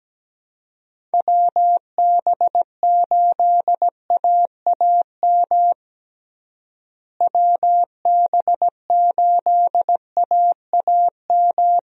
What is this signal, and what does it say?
Morse code, heard as a keyed tone, about 17 wpm, 700 Hz. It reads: WB8AAM WB8AAM
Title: USA Call Signs - 6 Characters - Repeated 2x - 17wpm